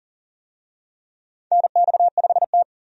Morse code, heard as a keyed tone, about 40 wpm, 700 Hz. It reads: NX5T
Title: USA Call Signs - 4 Characters - 40wpm